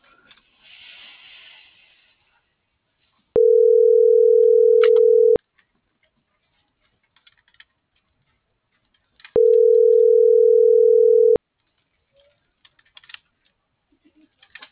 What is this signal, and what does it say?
Ringback tone (North America), 440 + 480 Hz.